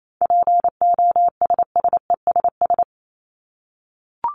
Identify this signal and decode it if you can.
Morse code, heard as a keyed tone, about 28 wpm, 700 Hz. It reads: POSSESS
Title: Top 1751-2000 Words 28wpm